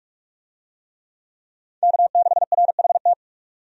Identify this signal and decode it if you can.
Morse code, heard as a keyed tone, about 46 wpm, 700 Hz. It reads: K6RHT
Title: USA Call Signs - 5 Characters - 45wpm